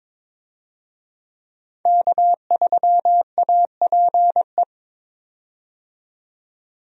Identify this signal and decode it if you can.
Morse code, heard as a keyed tone, about 22 wpm, 700 Hz. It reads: K3APE
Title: USA Call Signs - 5 Characters - 22wpm